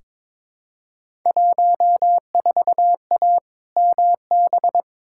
Morse code, heard as a keyed tone, about 22 wpm, 700 Hz. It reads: 14A MB
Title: Field Day Exchanges Encore - 22wpm